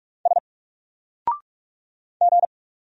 Morse code, heard as a keyed tone, about 43 wpm, 700 Hz. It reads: S G
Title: Single Character (Letter:Number:Punctuation:Prosign) Alt 45wpm